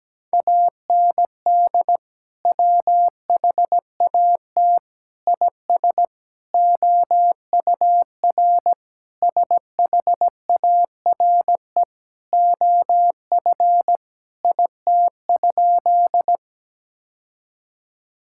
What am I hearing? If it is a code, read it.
Morse code, heard as a keyed tone, about 17 wpm, 700 Hz. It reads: AND WHAT IS OUR SHARE OF IT?
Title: Sentences from Top 1000 Words - Review Part 6 - 17wpm